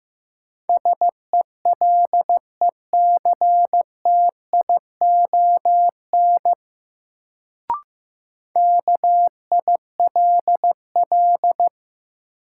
Morse code, heard as a keyed tone, about 15 wpm, 700 Hz. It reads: SELECTION KILL
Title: Sentences from Top 1400 Words - New Word Drill - 15wpm